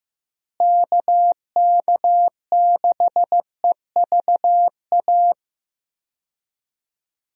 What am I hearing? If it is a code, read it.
Morse code, heard as a keyed tone, about 15 wpm, 700 Hz. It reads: KK6EVA
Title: USA Call Signs - 6 Characters - Repeated 2x - 15wpm